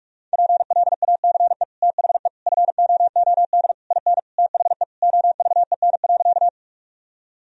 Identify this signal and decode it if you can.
Morse code, heard as a keyed tone, about 46 wpm, 700 Hz. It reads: PLACE THE FOOD IN THE OVEN.
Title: Sentences from Top 1000 Words - Review Part 4 - 45wpm